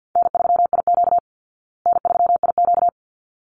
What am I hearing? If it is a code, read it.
Morse code, heard as a keyed tone, about 48 wpm, 700 Hz. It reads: N3IQ N3IQ
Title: Call Signs Repeated 3x - 50wpm